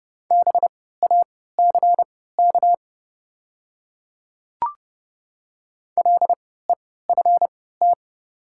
Morse code, heard as a keyed tone, about 31 wpm, spaced as for 17 wpm, 700 Hz. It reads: BACK LEFT
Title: 3x Character Spacing - Top 300 Words - 30wpm